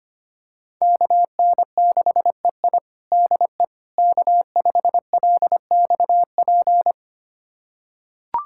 Morse code, heard as a keyed tone, about 25 wpm, 700 Hz. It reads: KN6EI DE K5LXP
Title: USA Callsign DE Callsign - 5 Characters - 25wpm